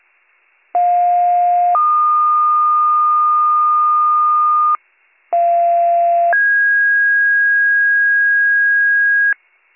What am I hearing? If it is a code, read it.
Two-tone page, heard as a one-second tone, then a three-second tone, about 710 then 1220 Hz, about 690 then 1650 Hz, with noise. It is heard from 0.7 to 9.3 s.